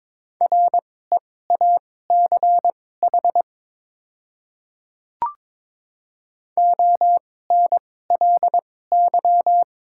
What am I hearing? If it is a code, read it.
Morse code, heard as a keyed tone, about 22 wpm, spaced as for 16 wpm, 700 Hz. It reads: REACH ONLY
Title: 2x Character Spacing - Top 500 Words - 22wpm